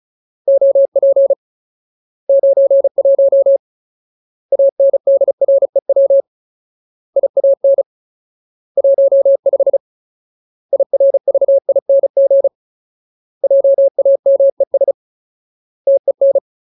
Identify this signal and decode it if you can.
Morse code, heard as a keyed tone, about 34 wpm, 550 Hz. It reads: OP 91 ANDREW IAN 15 IRVING JAMES TEN
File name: Ultimate Word List / 35WPM Ultimate